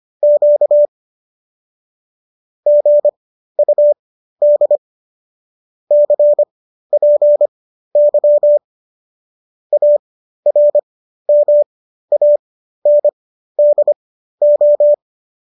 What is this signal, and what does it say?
Morse code, heard as a keyed tone, about 25 wpm, spaced as for 13 wpm, 590 Hz. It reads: Q GUD CPY ARMANDO